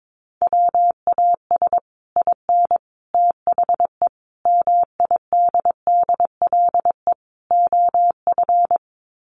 Morse code, heard as a keyed tone, about 22 wpm, 700 Hz. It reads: WAS IN THE MIDDLE OF
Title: Sets of 5 Words Encore 22wpm